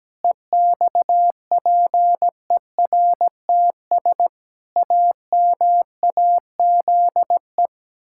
Morse code, heard as a keyed tone, about 17 wpm, 700 Hz. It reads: EXPERTS AMAZE
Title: Pangram Challenge v2 - 17wpm